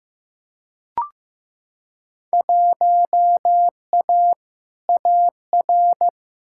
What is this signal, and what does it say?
Morse code, heard as a keyed tone, about 15 wpm, 700 Hz. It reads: 1A AR
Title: Field Day Exchanges Encore - 15wpm